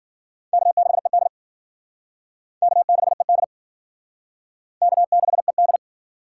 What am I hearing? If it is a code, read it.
Morse code, heard as a keyed tone, about 51 wpm, 700 Hz. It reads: K6ED K6ED K6ED